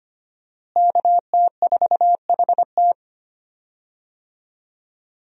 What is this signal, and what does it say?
Morse code, heard as a keyed tone, about 25 wpm, 700 Hz. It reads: KT4HT
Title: USA Call Signs - 5 Characters - Repeated 2x - 25wpm